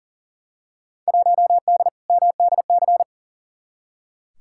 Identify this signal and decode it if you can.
Morse code, heard as a keyed tone, about 37 wpm, 700 Hz. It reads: 1D MDC
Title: Field Day Exchanges - 40wpm